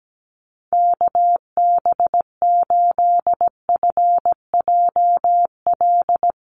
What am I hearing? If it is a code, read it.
Morse code, heard as a keyed tone, about 17 wpm, 700 Hz. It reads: KB8FJL
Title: USA Call Signs Repeated 3x 17wpm